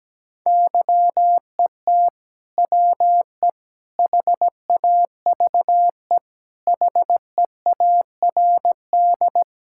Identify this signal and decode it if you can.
Morse code, heard as a keyed tone, about 17 wpm, 700 Hz. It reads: YET WE HAVE HEARD
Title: Sentences from Top 1000 Words - Review Part 5 - 17wpm